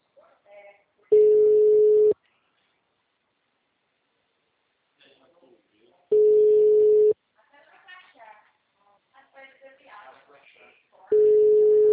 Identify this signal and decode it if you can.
Ringback tone (Europe, 425 Hz), with speech.